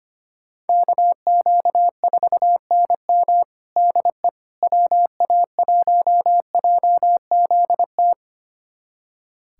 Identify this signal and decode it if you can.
Morse code, heard as a keyed tone, about 25 wpm, 700 Hz. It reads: KQ4NM DE WA1JZT
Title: USA Callsign DE Callsign 25wpm